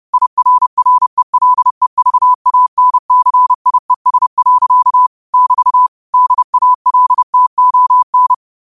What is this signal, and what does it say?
Morse code, heard as a keyed tone, about 30 wpm, 1000 Hz. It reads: IRRELEVANCIES. <BT> DALTON
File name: qotdfiles / 30wpm